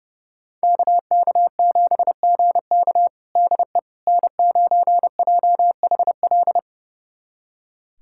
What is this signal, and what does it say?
Morse code, heard as a keyed tone, about 31 wpm, 700 Hz. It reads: KK7GK DE N9JHL
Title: USA Callsign DE Callsign - 5 Characters - 30wpm